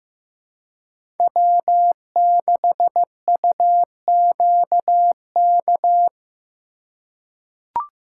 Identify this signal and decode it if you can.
Morse code, heard as a keyed tone, about 15 wpm, 700 Hz. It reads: W6UQK
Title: USA Call Signs - 4x Character Spacing - 15wpm